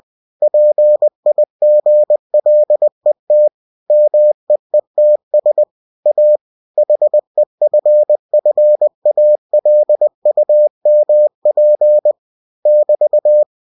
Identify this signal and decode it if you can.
Morse code, heard as a keyed tone, about 20 wpm, 600 Hz. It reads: PIGLET MEETS A HEFFALUMP <BT>
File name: cw-books / winnie-the-pooh / winnie-20wpm-page-0033